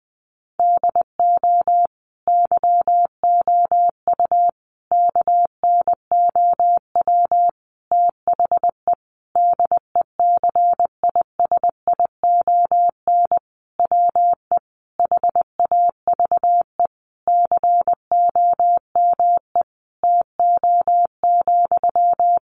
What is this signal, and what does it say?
Morse code, heard as a keyed tone, about 20 wpm, 700 Hz. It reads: DO YOU KNOW THE DECISION WE HAVE COME TO,
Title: Sentences from Top 1000 Words 20wpm